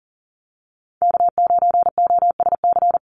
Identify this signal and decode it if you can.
Morse code, heard as a keyed tone, about 38 wpm, 700 Hz. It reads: K9OSC
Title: NA SOTA Chaser Callsigns 40wpm